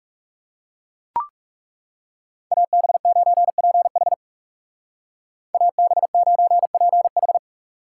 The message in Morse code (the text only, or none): AB9PH AB9PH